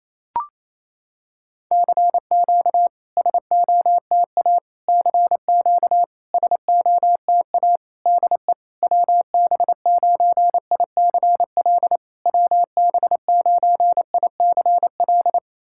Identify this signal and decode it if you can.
Morse code, heard as a keyed tone, about 28 wpm, 700 Hz. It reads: CQ SOTA CQ SOTA DE WB9ICL WB9ICL
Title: SOTA QSO Practice 28wpm